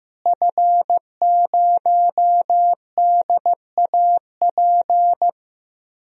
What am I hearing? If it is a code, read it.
Morse code, heard as a keyed tone, about 15 wpm, 700 Hz. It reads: F0DAP
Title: DXCC-100 Call Signs - 5 Characters - Repeated 2x - 15wpm